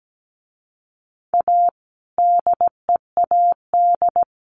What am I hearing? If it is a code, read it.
Morse code, heard as a keyed tone, about 17 wpm, 700 Hz. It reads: A DEAD
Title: A Sound Of Thunder 17wpm